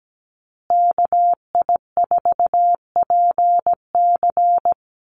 Morse code, heard as a keyed tone, about 17 wpm, 700 Hz. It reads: KI4PC
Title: USA Call Signs Repeated 3x 17wpm